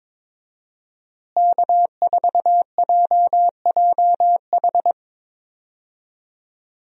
Morse code, heard as a keyed tone, about 22 wpm, 700 Hz. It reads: K4JJH
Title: USA Call Signs - 5 Characters - Repeated 2x - 22wpm